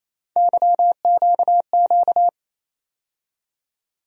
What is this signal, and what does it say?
Morse code, heard as a keyed tone, about 28 wpm, 700 Hz. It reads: YQQ
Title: USA Call Sign Postfixes - 3 Letters 28wpm